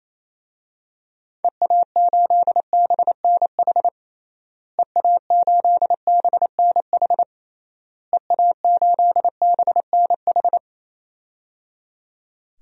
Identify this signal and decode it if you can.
Morse code, heard as a keyed tone, about 28 wpm, 700 Hz. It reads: EA8BNH EA8BNH EA8BNH
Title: DXCC-100 Call Signs - 6 Characters - Repeated 3x - 28wpm